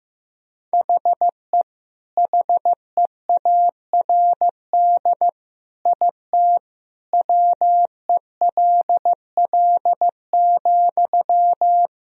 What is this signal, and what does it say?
Morse code, heard as a keyed tone, about 15 wpm, 700 Hz. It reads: HE HEARD IT WELL,
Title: Sentences from Top 1000 Words - Review Part 3 - 15wpm